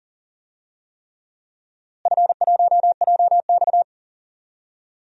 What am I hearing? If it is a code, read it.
Morse code, heard as a keyed tone, about 40 wpm, 700 Hz. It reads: F1JX
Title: DXCC-100 Call Signs - 4 Characters - Repeated 3x - 40wpm